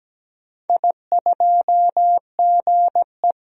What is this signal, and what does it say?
Morse code, heard as a keyed tone, about 17 wpm, 700 Hz. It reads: I2GE
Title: DXCC-100 Call Signs - 4 Characters - 17wpm